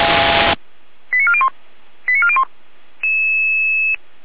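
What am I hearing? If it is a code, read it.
Five-tone selective call (ZVEI), heard as quick runs of five tones, tones of about 70 ms. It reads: 89401 89401
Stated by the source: Five-tone ZVEI selective call received by SDR, from the Feuerwehr Orlamünde (f85360) audio archive.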